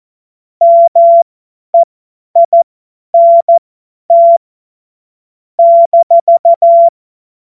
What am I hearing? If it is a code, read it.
Morse code, heard as a keyed tone, about 14 wpm, spaced as for 10 wpm, 680 Hz. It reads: MEINT -